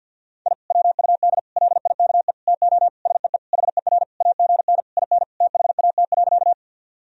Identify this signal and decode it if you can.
Morse code, heard as a keyed tone, about 52 wpm, 700 Hz. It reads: I WUD LIKE TO SEE HER AGN IN THAT.
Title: Sentences from Top 200 Words and Abbreviations 50wpm